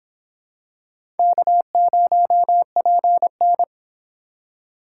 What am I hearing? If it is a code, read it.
Morse code, heard as a keyed tone, about 26 wpm, 700 Hz. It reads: K0PN